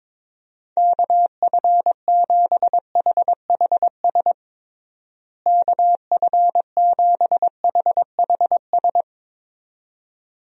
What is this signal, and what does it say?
Morse code, heard as a keyed tone, about 21 wpm, 700 Hz. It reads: KF7HHS KF7HHS
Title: USA Call Signs - 6 Characters - Repeated 2x - 22wpm